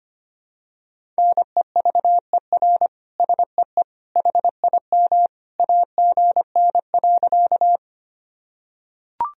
Morse code, heard as a keyed tone, about 25 wpm, 700 Hz. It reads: NEVER SEE HIM AGN.
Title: Sentences from Top 300 Words and Abbreviations 25wpm